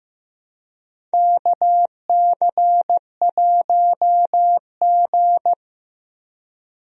Morse code, heard as a keyed tone, about 15 wpm, 700 Hz. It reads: KC1G
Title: USA Call Signs - 4 Characters - Repeated 3x - 15wpm